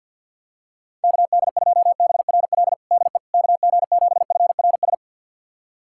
Morse code, heard as a keyed tone, about 47 wpm, 700 Hz. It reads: KD2BRL DE KG7FRS
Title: USA Callsign DE Callsign 50wpm